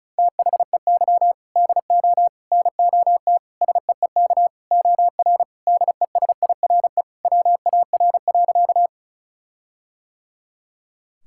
Morse code, heard as a keyed tone, about 34 wpm, 700 Hz. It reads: THEY DO NOT SEEK OR DESIRE WAR.
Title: Sentences from Top 1600 Words 35wpm